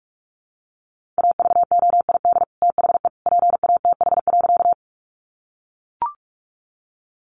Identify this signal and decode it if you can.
Morse code, heard as a keyed tone, about 44 wpm, 700 Hz. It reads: AVOID THE PATH.
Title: Sentences from Top 1400 Words 45wpm